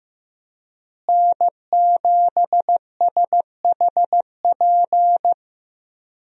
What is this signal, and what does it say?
Morse code, heard as a keyed tone, about 15 wpm, 700 Hz. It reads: N7SHP
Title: USA Call Signs - 5 Characters - Repeated 2x - 15wpm